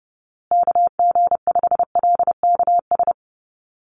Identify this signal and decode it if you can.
Morse code, heard as a keyed tone, about 30 wpm, 700 Hz. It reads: KG5LKS
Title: USA Call Signs Repeated 3x 30wpm